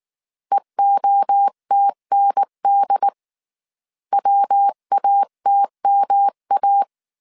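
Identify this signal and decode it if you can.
Morse code, heard as a keyed tone, about 19 wpm, 800 Hz. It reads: EOTND WATMA
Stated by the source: Morse code dictation at 18 wpm from the IW3SGT QRS Morse code training course, five-character groups of 300 characters.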